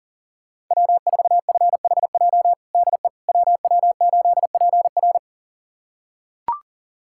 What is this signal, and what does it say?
Morse code, heard as a keyed tone, about 40 wpm, 700 Hz. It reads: W4FHJ DE WW8PR
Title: USA Callsign DE Callsign - 5 Characters - 40wpm